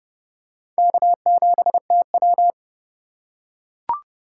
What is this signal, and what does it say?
Morse code, heard as a keyed tone, about 30 wpm, 700 Hz. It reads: K7TW
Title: USA Call Signs - 4 Characters - 4x Character Spacing - 30wpm